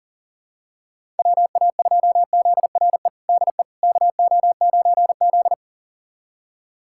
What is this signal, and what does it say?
Morse code, heard as a keyed tone, about 40 wpm, 700 Hz. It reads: WA2ZRE DE KO9Z
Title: USA Callsign DE Callsign 40wpm